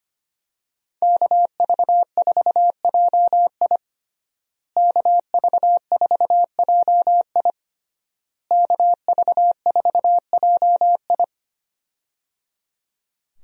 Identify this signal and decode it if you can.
Morse code, heard as a keyed tone, about 25 wpm, 700 Hz. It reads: KV4JI KV4JI KV4JI
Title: USA Call Signs - 5 Characters - Repeated 3x - 25wpm